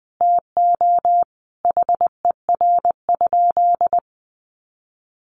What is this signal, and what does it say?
Morse code, heard as a keyed tone, about 20 wpm, 700 Hz. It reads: TO HER?
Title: Sentences from Top 1400 Words 20wpm